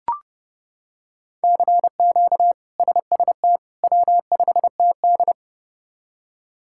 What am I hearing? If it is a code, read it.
Morse code, heard as a keyed tone, about 30 wpm, 700 Hz. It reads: CQ SST W5TD
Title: K1USN SST QSO Practice - 30wpm